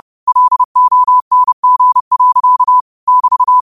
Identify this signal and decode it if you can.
Morse code, heard as a keyed tone, about 30 wpm, 1000 Hz. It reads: LONG. <BT>
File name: qotdfiles / 30wpm